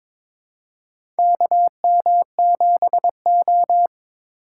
Morse code, heard as a keyed tone, about 22 wpm, 700 Hz. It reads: KM7O
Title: USA Call Signs - 4x Character Spacing - 22wpm